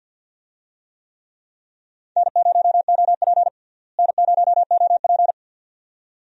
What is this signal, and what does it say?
Morse code, heard as a keyed tone, about 47 wpm, 700 Hz. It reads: N0OP N0OP
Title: USA Call Signs - 4 Characters - Repeated 2x - 50wpm